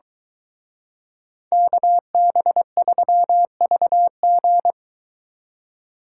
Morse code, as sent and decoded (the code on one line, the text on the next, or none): -.- -... ...-- ...- --.
KB3VG